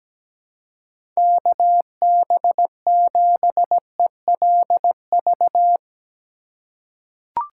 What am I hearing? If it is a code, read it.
Morse code, heard as a keyed tone, about 17 wpm, 700 Hz. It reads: KB7ELV
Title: USA Call Signs - 6 Characters - 2x Character Spacing - 17wpm